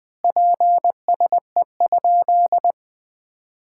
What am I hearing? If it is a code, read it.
Morse code, heard as a keyed tone, about 20 wpm, 700 Hz. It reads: PSE?
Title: Common QSO Phrases - 4x Word-Spacing - 20wpm